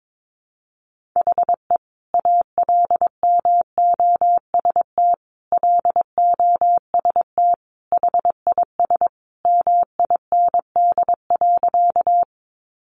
Morse code, heard as a keyed tone, about 22 wpm, 700 Hz. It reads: HE ALMOST LOST HIS MIND.
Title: Sentences from Top 1400 Words 22wpm